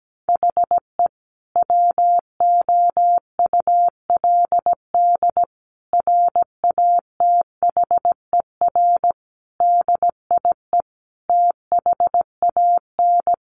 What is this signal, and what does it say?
Morse code, heard as a keyed tone, about 17 wpm, 700 Hz. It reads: HE WOULD RATHER DIE THAN
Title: Sentences from Top 1500 Words 17wpm